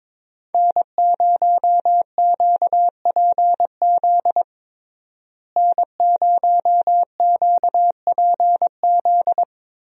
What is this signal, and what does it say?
Morse code, heard as a keyed tone, about 22 wpm, 700 Hz. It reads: N0QPZ N0QPZ
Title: USA Call Signs - 5 Characters - Repeated 3x - 22wpm